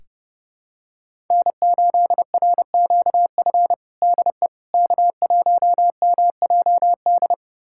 Morse code, heard as a keyed tone, about 30 wpm, 700 Hz. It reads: N8RQF DE K1MJD
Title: USA Callsign DE Callsign - 5 Characters - 30wpm